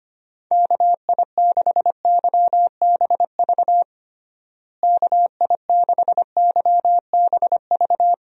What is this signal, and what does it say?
Morse code, heard as a keyed tone, about 25 wpm, 700 Hz. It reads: KI6YBV KI6YBV
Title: USA Call Signs - 6 Characters - Repeated 3x - 25wpm